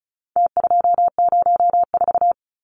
Morse code, heard as a keyed tone, about 34 wpm, 700 Hz. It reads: T204
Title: Numbers - Contest Style - 4-Digit 35wpm